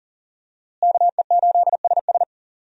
Morse code, heard as a keyed tone, about 40 wpm, 700 Hz. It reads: KE8SS
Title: USA Call Signs - 5 Characters - Repeated 2x - 40wpm